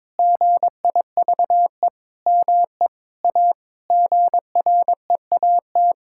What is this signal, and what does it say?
Morse code, heard as a keyed tone, about 22 wpm, 700 Hz. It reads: GIVE ME A GREAT
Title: Sentences from Top 1600 Words 22wpm